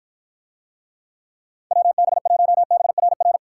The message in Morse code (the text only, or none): WB1BRR